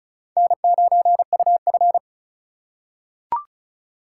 Morse code, heard as a keyed tone, about 34 wpm, 700 Hz. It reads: N9UF
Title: USA Call Signs - 3x Character Spacing - 35wpm